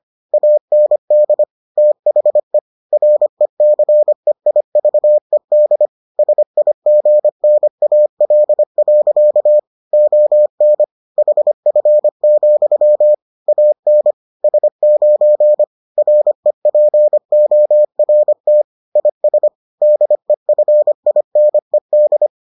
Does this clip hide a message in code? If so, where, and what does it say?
Yes: Morse code from 0.3 to 22.3 s, heard as a keyed tone, about 25 wpm, 600 Hz. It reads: AND THE RECEIVED SIGNAL. ON HF, AN S9 REPORT IS DEFINED